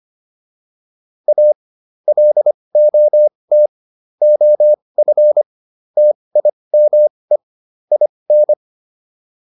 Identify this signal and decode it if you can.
Morse code, heard as a keyed tone, about 25 wpm, spaced as for 20 wpm, 600 Hz. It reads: A LOT OF TIME IN